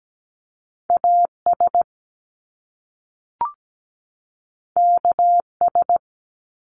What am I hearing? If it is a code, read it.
Morse code, heard as a keyed tone, about 17 wpm, 700 Hz. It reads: AS KS
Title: Top Two Letter Combinations in Words 17wpm